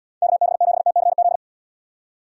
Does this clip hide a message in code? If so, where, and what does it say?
Yes: Morse code from 0.2 to 1.4 s, heard as a keyed tone, about 79 wpm, 700 Hz. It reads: KB7TCZ